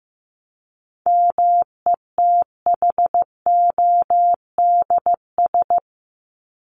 Morse code, heard as a keyed tone, about 15 wpm, 700 Hz. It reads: METHODS
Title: ES:S Suffix - Top 100 Words with Base Words 15wpm